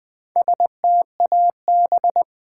Morse code, heard as a keyed tone, about 21 wpm, 700 Hz. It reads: STAB
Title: Build A Word - 2 Letters At A Time - 4-Letter Words - 20wpm